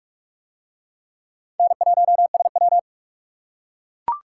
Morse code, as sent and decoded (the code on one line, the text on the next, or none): -. .---- ... .--
N1SW